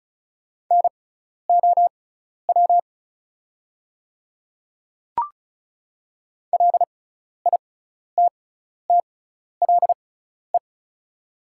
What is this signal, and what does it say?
Morse code, heard as a keyed tone, about 36 wpm, spaced as for 12 wpm, 700 Hz. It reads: NOW LITTLE
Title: 6x Character Spacing - Top 200 Words - 35wpm